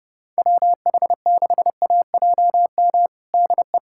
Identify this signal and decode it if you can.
Morse code, heard as a keyed tone, about 30 wpm, 700 Hz. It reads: WH6AJM DE
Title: USA Callsign DE Callsign 30wpm